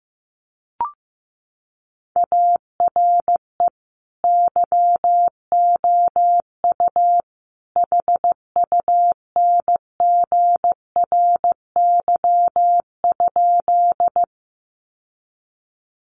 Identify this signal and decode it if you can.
Morse code, heard as a keyed tone, about 15 wpm, 700 Hz. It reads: ARE YOU HUNGRY?
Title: Sentences from Top 1400 Words 15wpm